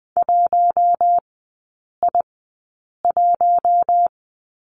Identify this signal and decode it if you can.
Morse code, heard as a keyed tone, about 20 wpm, spaced as for 8 wpm, 700 Hz. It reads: 1I1
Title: 047 Lesson 9e - Number 4 - Three Characters 20wpm